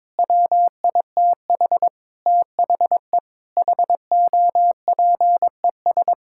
Morse code, heard as a keyed tone, about 21 wpm, 700 Hz. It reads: WITH THE HOPES
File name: Sets of 9 Words - 22wpm